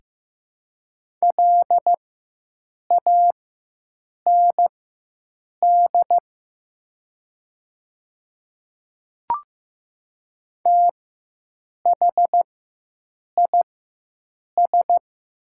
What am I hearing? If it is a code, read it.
Morse code, heard as a keyed tone, about 15 wpm, spaced as for 7 wpm, 700 Hz. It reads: LAND THIS
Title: 4x Character Spacing - Top 200 Words - 15wpm